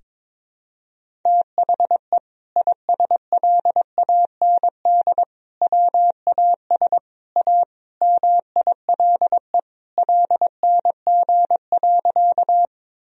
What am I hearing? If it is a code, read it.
Morse code, heard as a keyed tone, about 22 wpm, 700 Hz. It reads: THE ISLAND WAS A MILE LNG.
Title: Sentences from Top 500 Words and Abbreviations 22wpm